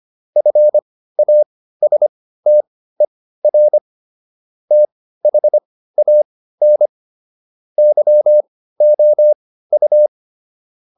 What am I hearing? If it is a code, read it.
Morse code, heard as a keyed tone, about 25 wpm, spaced as for 15 wpm, 600 Hz. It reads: FASTER THAN YOU